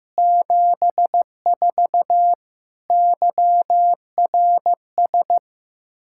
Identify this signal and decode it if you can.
Morse code, heard as a keyed tone, about 15 wpm, 700 Hz. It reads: 74 YRS
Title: Practice QSO - Exchange 3 - More Info - 15wpm